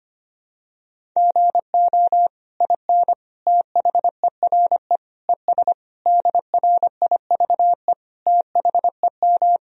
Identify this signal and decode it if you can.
Morse code, heard as a keyed tone, about 25 wpm, 700 Hz. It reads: GO IN THERE ES DRIVE THEM
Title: Sentences from Top 500 Words and Abbreviations 25wpm